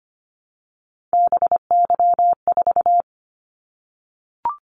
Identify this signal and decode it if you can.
Morse code, heard as a keyed tone, about 25 wpm, 700 Hz. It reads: BY4
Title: Callsign Prefixes for Top 100 DXCC - 25wpm